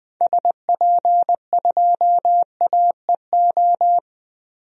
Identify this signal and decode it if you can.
Morse code, heard as a keyed tone, about 20 wpm, 700 Hz. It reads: SP2AEO
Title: DXCC-100 Call Signs - 6 Characters - 20wpm